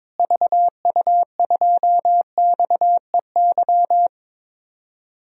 Morse code, heard as a keyed tone, about 21 wpm, 700 Hz. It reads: VU2XEY
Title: DXCC-100 Call Signs - 6 Characters - Repeated 2x - 22wpm